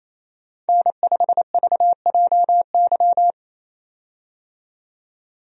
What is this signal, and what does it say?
Morse code, heard as a keyed tone, about 28 wpm, 700 Hz. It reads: N5VJY
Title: USA Call Signs - 5 Characters - 28wpm